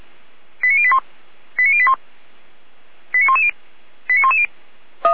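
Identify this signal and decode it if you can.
Five-tone selective call (ZVEI), heard as quick runs of five tones, tones of about 70 ms. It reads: 89081 89081 89110 89110
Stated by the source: ZVEI five-tone call 89140 received by SDR, from the feuerwehr-orlamuende.org fire brigade server.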